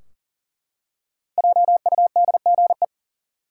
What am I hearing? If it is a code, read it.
Morse code, heard as a keyed tone, about 40 wpm, 700 Hz. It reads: JUDGE
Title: Five Letter Words - 40wpm